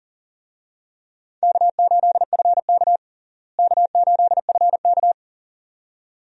Morse code, heard as a keyed tone, about 40 wpm, 700 Hz. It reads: K8FK K8FK
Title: USA Call Signs - 4 Characters - Repeated 2x - 40wpm